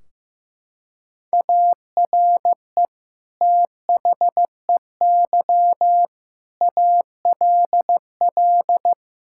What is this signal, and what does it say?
Morse code, heard as a keyed tone, about 16 wpm, 700 Hz. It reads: ARE THEY ALL
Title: Sentences from Top 1600 Words 15wpm